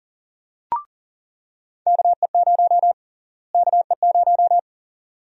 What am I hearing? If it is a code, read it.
Morse code, heard as a keyed tone, about 39 wpm, 700 Hz. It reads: KE0 KE0
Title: USA Call Sign Prefixes - Repeated 2x - 40wpm